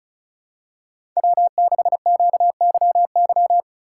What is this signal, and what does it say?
Morse code, heard as a keyed tone, about 36 wpm, 700 Hz. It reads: W6QYY
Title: USA Call Signs - 5 Characters - 2x Character Spacing - 35wpm